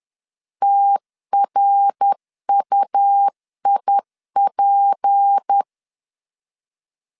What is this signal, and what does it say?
Morse code, heard as a keyed tone, about 10 wpm, 800 Hz. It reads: TRUIP
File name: CW / DET / 1 6 1500C S C 10wpm 4252